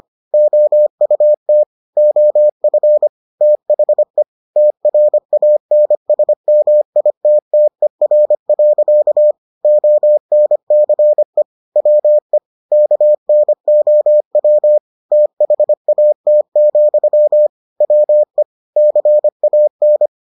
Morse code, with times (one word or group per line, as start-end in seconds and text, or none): 0.3-1.6 OUT
2.0-3.1 OF
3.4-4.2 THE
4.6-9.3 TRANSMITTER.
9.6-11.4 ONCE
11.8-12.4 WE
12.7-14.8 KNOW
15.1-17.5 THAT,
17.8-18.4 WE
18.8-20.1 CAN